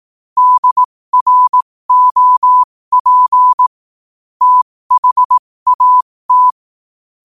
Morse code, heard as a keyed tone, about 18 wpm, spaced as for 15 wpm, 1000 Hz. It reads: DROP THAT
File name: qotdfiles / 15wpm